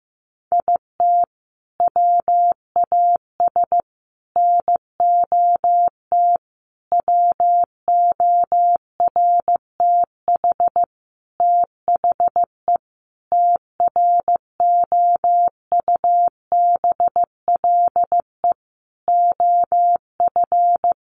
Morse code, heard as a keyed tone, about 15 wpm, 700 Hz. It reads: IT WAS NOT WORTH THE TROUBLE OF
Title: Sentences from Top 1200 Words 15wpm